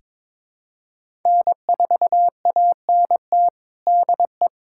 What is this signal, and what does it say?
Morse code, heard as a keyed tone, about 22 wpm, 700 Hz. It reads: N4ANT DE
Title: USA Callsign DE Callsign - 5 Characters - 22wpm